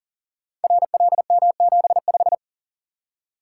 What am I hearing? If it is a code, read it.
Morse code, heard as a keyed tone, about 38 wpm, 700 Hz. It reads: RLM75
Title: Code Groups - 5 Letter:Numbers - 40wpm